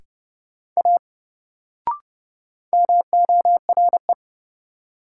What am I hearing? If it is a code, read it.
Morse code, heard as a keyed tone, about 30 wpm, 700 Hz. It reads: A MORE
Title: Sentences from Top 100 Words - New Word Drill - 30wpm